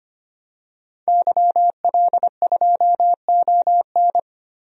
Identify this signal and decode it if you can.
Morse code, heard as a keyed tone, about 25 wpm, 700 Hz. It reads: YL2ON
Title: DXCC-100 Call Signs - 5 Characters - Repeated 2x - 25wpm